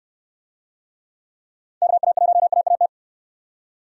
Morse code, heard as a keyed tone, about 69 wpm, 700 Hz. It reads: KA1AAT